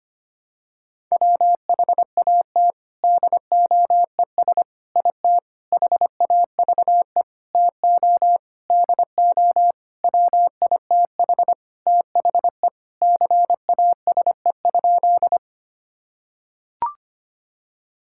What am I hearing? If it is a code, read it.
Morse code, heard as a keyed tone, about 25 wpm, 700 Hz. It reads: WHAT DOES IT HAVE TO DO WITH THE CASE?
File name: practice / 25wpm / Sentences with 4-Word Phrases - 25wpm